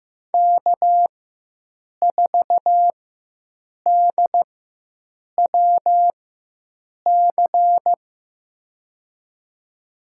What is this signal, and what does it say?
Morse code, heard as a keyed tone, about 15 wpm, spaced as for 7 wpm, 700 Hz. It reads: K4DWC
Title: USA Call Signs - 5 Characters - 4x Character Spacing - 15wpm